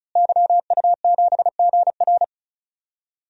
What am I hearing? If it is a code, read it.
Morse code, heard as a keyed tone, about 35 wpm, 700 Hz. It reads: YU7GR
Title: DXCC-100 Call Signs - 5 Characters - Repeated 2x - 35wpm